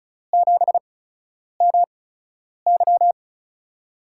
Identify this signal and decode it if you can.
Morse code, heard as a keyed tone, about 34 wpm, spaced as for 10 wpm, 700 Hz. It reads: 7MY
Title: USA Call Signs - 5 Characters - 8x Character Spacing - 35wpm